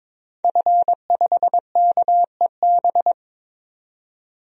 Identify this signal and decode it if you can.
Morse code, heard as a keyed tone, about 22 wpm, 700 Hz. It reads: F5KEB
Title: DXCC-100 Call Signs - 5 Characters - 22wpm